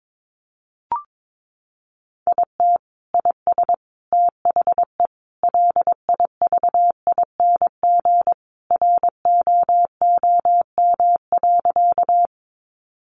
Morse code, heard as a keyed tone, about 22 wpm, 700 Hz. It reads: IT IS THE LIVING ROOM.
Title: Sentences from Top 1100 Words 22wpm